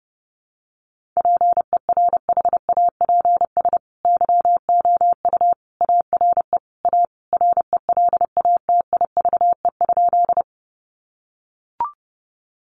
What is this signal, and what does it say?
Morse code, heard as a keyed tone, about 30 wpm, 700 Hz. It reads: PERHAPS YOU ARE A RELATIVE?
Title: Sentences from Top 1400 Words 30wpm